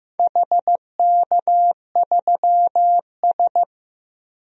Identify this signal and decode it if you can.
Morse code, heard as a keyed tone, about 15 wpm, 700 Hz. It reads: HK3S
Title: DXCC-100 Call Signs - 4 Characters - 15wpm